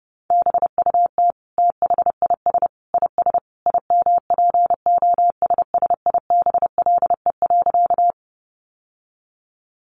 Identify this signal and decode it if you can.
Morse code, heard as a keyed tone, about 30 wpm, 700 Hz. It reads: BUT THIS IS IMPOSSIBLE.
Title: Sentences from Top 1100 Words 30wpm